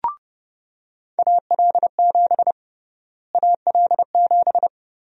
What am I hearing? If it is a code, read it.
Morse code, heard as a keyed tone, about 32 wpm, 700 Hz. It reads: AL7 AL7
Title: USA Call Sign Prefixes - Repeated 2x - 30wpm